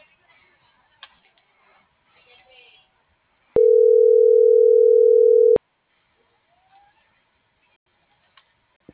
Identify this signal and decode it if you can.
Ringback tone (North America), 440 + 480 Hz.